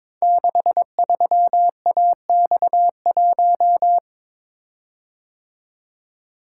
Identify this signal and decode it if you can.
Morse code, heard as a keyed tone, about 23 wpm, 700 Hz. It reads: 63AX1
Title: Code Groups - 5 Letter:Numbers - 22wpm